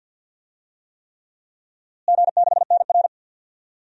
Morse code, heard as a keyed tone, about 51 wpm, 700 Hz. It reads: G6NR
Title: DXCC-100 Call Signs - 4 Characters - Repeated 3x - 50wpm